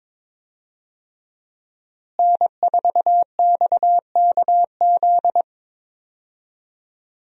Morse code, heard as a keyed tone, about 22 wpm, 700 Hz. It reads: N4XKZ
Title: USA Call Signs - 5 Characters - Repeated 2x - 22wpm